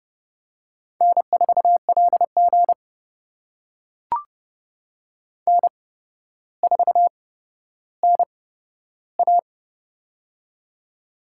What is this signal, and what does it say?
Morse code, heard as a keyed tone, about 30 wpm, spaced as for 13 wpm, 700 Hz. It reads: N4LG N4NA
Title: USA Call Signs - 4 Characters - 8x Character Spacing - 30wpm